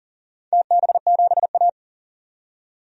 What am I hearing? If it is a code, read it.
Morse code, heard as a keyed tone, about 42 wpm, 700 Hz. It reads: TB7A
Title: DXCC-100 Call Signs - 4 Characters - 40wpm